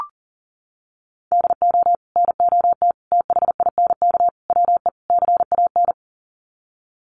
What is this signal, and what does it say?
Morse code, heard as a keyed tone, about 40 wpm, 700 Hz. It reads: DO NOT THINK WE CAN
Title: Sets of 5 Words Encore2 40wpm